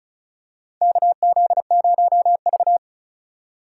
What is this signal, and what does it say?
Morse code, heard as a keyed tone, about 34 wpm, 700 Hz. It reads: KZ0V
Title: USA Call Signs - 4 Characters - 4x Character Spacing - 35wpm